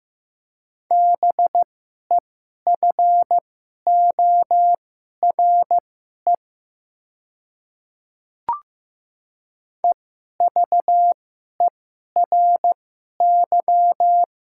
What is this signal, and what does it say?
Morse code, heard as a keyed tone, about 16 wpm, spaced as for 11 wpm, 700 Hz. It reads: BEFORE EVERY
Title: 2x Character Spacing - Top 200 Words - 15wpm